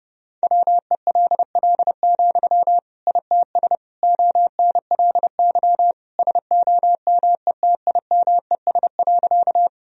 Morse code, heard as a keyed tone, about 30 wpm, 700 Hz. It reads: WELL, ITS ONLY SOMETIMES.
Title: Sentences from Top 1000 Words - Review Part 3 - 30wpm